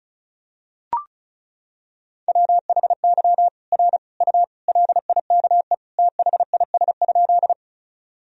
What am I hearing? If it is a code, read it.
Morse code, heard as a keyed tone, about 36 wpm, 700 Hz. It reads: WHY R U LIKE THIS?
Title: Sentences from Top 200 Words and Abbreviations 35wpm